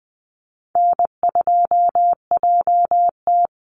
Morse code, heard as a keyed tone, about 20 wpm, 700 Hz. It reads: N2JT
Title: Easy Call Signs Repeated 3x - 20wpm